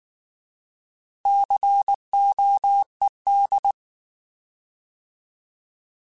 Morse code, heard as a keyed tone, about 19 wpm, 800 Hz. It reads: COED
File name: Alphabetical / Alphabetical mp3s / Cs @18